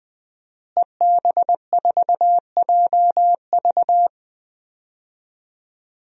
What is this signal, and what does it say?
Morse code, heard as a keyed tone, about 20 wpm, 700 Hz. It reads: EB4JV
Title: DXCC-100 Call Signs - 5 Characters - Repeated 3x - 20wpm